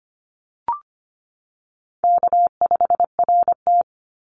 Morse code, heard as a keyed tone, about 25 wpm, 700 Hz. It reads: K5RT
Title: Easy Call Signs - 25wpm